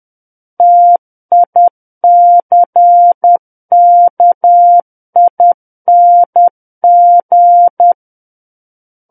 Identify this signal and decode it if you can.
Morse code, heard as a keyed tone, about 10 wpm, 700 Hz. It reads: TICKING
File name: Book One / 01-Chapter 1 - 10 WPM